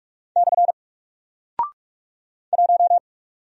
Morse code, heard as a keyed tone, about 44 wpm, 700 Hz. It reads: / 1
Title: Single Character (Letter:Number:Punctuation:Prosign) Alt 45wpm